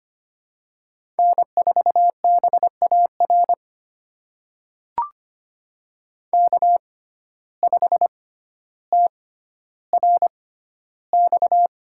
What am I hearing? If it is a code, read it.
Morse code, heard as a keyed tone, about 25 wpm, spaced as for 13 wpm, 700 Hz. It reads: N4BAR K5TRX
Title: USA Call Signs - 5 Characters - 6x Character Spacing - 25wpm